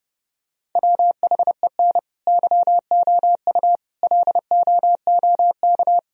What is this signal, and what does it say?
Morse code, heard as a keyed tone, about 29 wpm, 700 Hz. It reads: WHEN YOU LOOK